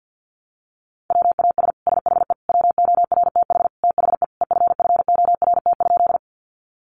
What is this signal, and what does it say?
Morse code, heard as a keyed tone, about 50 wpm, 700 Hz. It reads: WAS SHE WORTH THE EFFORT?
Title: Sentences from Top 1200 Words 50wpm